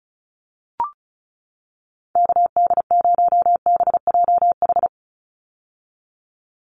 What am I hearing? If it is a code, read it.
Morse code, heard as a keyed tone, about 34 wpm, 700 Hz. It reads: KD0BJH
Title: USA Call Signs 35wpm